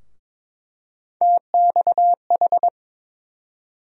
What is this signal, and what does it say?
Morse code, heard as a keyed tone, about 22 wpm, 700 Hz. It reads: TXH
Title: USA Call Sign Postfixes - 3 Letters 22wpm